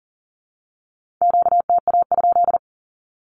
Morse code, heard as a keyed tone, about 41 wpm, 700 Hz. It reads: QTA?
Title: Q-Codes with Spoken Meaning - 40wpm